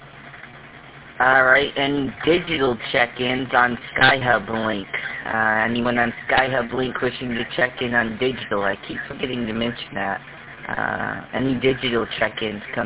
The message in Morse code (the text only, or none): none